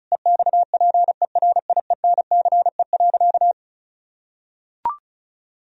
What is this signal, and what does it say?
Morse code, heard as a keyed tone, about 36 wpm, 700 Hz. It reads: EXPERIENCE.
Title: Sentences from Top 1000 Words - Review Part 4 - 35wpm